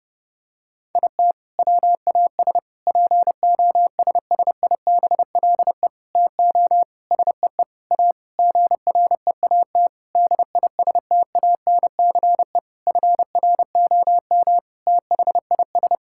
Morse code, heard as a keyed tone, about 30 wpm, 700 Hz. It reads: IT WAS POSSIBLE TO SEE A GREAT DISTANCE FROM THIS
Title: Sentences from Top 1600 Words 30wpm